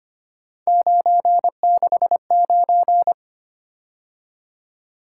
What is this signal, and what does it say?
Morse code, heard as a keyed tone, about 25 wpm, 700 Hz. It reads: 969
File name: Three Letter-Number Combinations - 25wpm